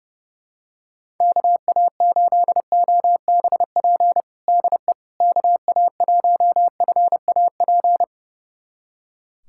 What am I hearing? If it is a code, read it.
Morse code, heard as a keyed tone, about 30 wpm, 700 Hz. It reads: KA8OBP DE KA1FAP
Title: USA Callsign DE Callsign - 6 Characters - 30wpm